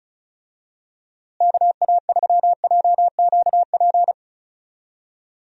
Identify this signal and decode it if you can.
Morse code, heard as a keyed tone, about 34 wpm, 700 Hz. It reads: KA3JQP
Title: USA Call Signs - 6 Characters - Repeated 3x - 35wpm